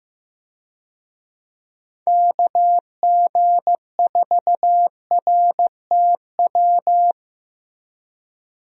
Morse code, heard as a keyed tone, about 15 wpm, 700 Hz. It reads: KG4RTW
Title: USA Call Signs - 6 Characters - 15wpm